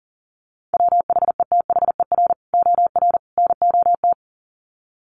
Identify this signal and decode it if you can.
Morse code, heard as a keyed tone, about 40 wpm, 700 Hz. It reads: WHETHER OR NOT
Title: Sets of 3 Words 40wpm